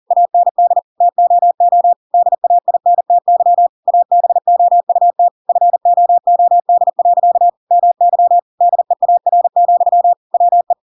Morse code, heard as a keyed tone, about 40 wpm, 700 Hz. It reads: AND TOO DAINTY ABOUT FOOD. MY DEAR, WE
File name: Pinocchio / Pinocchio ch7 40 40 ik1hge